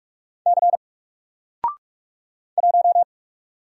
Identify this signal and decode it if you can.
Morse code, heard as a keyed tone, about 44 wpm, 700 Hz. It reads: C 1
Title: Single Character (Letter:Number:Punctuation:Prosign) 45wpm